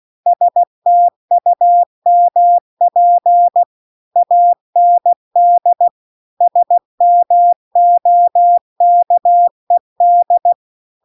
Morse code, heard as a keyed tone, about 16 wpm, 700 Hz. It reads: STUMP AND SMOKED